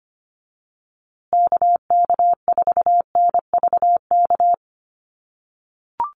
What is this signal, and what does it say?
Morse code, heard as a keyed tone, about 25 wpm, 700 Hz. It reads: KK4NVK
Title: USA Call Signs Repeated 2x 25wpm